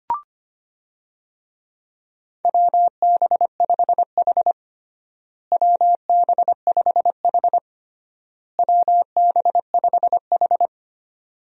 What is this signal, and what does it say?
Morse code, heard as a keyed tone, about 25 wpm, 700 Hz. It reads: WB5H WB5H WB5H
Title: USA Call Signs - 4 Characters - Repeated 3x - 25wpm